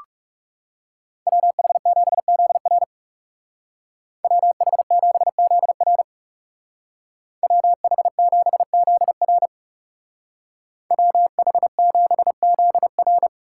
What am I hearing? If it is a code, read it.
Morse code, heard as a keyed tone, about 37 wpm, 700 Hz. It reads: WH7ZR WH7ZR WH7ZR WH7ZR
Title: Speed Racing - USA 2x2 Call Signs 30wpm